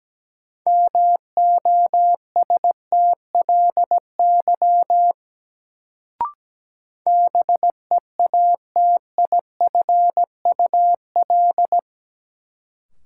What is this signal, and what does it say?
Morse code, heard as a keyed tone, about 17 wpm, 700 Hz. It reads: MOSTLY BEATIFUL
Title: Sentences from Top 1300 Words - New Word Drill - 17wpm